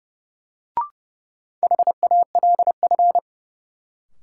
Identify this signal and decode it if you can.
Morse code, heard as a keyed tone, about 30 wpm, 700 Hz. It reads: HALF
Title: Sets of 3 Words - New Word Drill - 30wpm